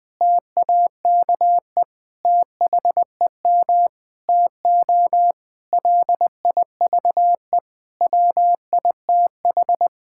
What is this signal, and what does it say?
Morse code, heard as a keyed tone, about 19 wpm, 700 Hz. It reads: TAKE THEM TO LIVE WITH
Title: Sentences from Top 1600 Words 20wpm